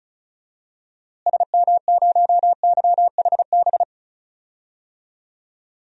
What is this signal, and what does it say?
Morse code, heard as a keyed tone, about 34 wpm, 700 Hz. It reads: SM0YHB
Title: DXCC-100 Call Signs - 6 Characters - Repeated 2x - 35wpm